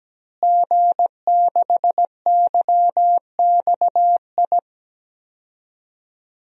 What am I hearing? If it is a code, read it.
Morse code, heard as a keyed tone, about 17 wpm, 700 Hz. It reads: G6YXI
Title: DXCC-100 Call Signs - 5 Characters - Repeated 3x - 17wpm